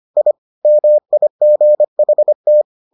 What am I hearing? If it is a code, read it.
Morse code, heard as a keyed tone, about 25 wpm, 600 Hz. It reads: I MIGHT